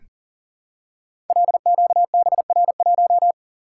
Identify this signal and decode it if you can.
Morse code, heard as a keyed tone, about 38 wpm, 700 Hz. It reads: LQBR1
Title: Code Groups - 5 Letter:Numbers - 40wpm